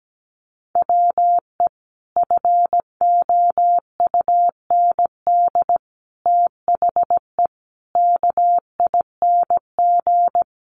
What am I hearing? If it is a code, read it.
Morse code, heard as a keyed tone, about 17 wpm, 700 Hz. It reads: WE FOUND THE KING
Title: Sentences from Top 1300 Words 17wpm